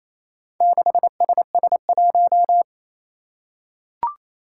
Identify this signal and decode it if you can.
Morse code, heard as a keyed tone, about 27 wpm, 700 Hz. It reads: 6SS1
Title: Four Challenging Letter-Number Combinations 28wpm